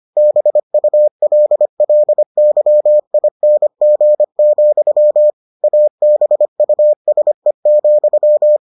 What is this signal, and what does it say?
Morse code, heard as a keyed tone, about 25 wpm, 600 Hz. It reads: BULLYING, ABUSE,